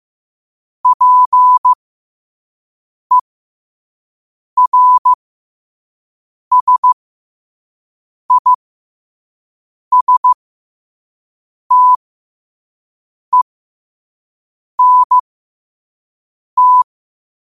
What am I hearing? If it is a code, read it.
Morse code, heard as a keyed tone, about 15 wpm, spaced as for 5 wpm, 1000 Hz. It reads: PERSISTENT
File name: qotdfiles / 5wpm